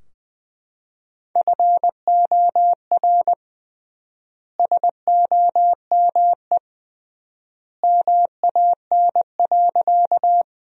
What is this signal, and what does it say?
Morse code, heard as a keyed tone, about 21 wpm, 700 Hz. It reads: FOR SOME MAN.
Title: 3x Word-Spacing - Sentences from Top 200 Words - 20wpm